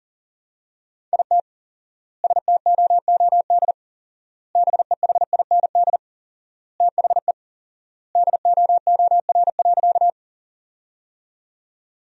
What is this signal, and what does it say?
Morse code, heard as a keyed tone, about 40 wpm, 700 Hz. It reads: IT STOOD BEHIND THE DOOR.
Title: 4x Word-Spacing - Sentences from Top 500 Words - 40wpm